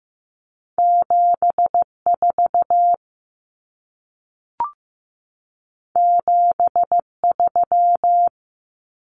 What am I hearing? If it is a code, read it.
Morse code, heard as a keyed tone, about 15 wpm, 700 Hz. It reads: 74 73
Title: Common Cut Numbers - 2-Digit 15wpm